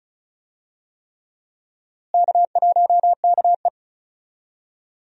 Morse code, heard as a keyed tone, about 34 wpm, 700 Hz. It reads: K1KE
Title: USA Call Signs - 4 Characters - Repeated 3x - 35wpm